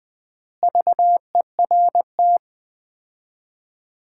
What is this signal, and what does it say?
Morse code, heard as a keyed tone, about 20 wpm, 700 Hz. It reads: VERT
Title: Common QSO Phrases - 3x Word-Spacing - 20wpm